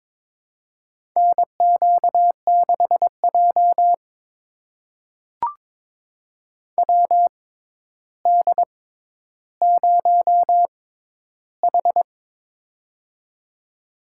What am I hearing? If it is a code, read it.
Morse code, heard as a keyed tone, about 22 wpm, spaced as for 11 wpm, 700 Hz. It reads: NQ6J WD0H
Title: USA Call Signs - 4 Characters - 6x Character Spacing - 22wpm